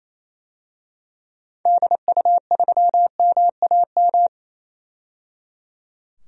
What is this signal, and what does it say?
Morse code, heard as a keyed tone, about 28 wpm, 700 Hz. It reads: DU3MAM
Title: DXCC-100 Call Signs - 6 Characters - 28wpm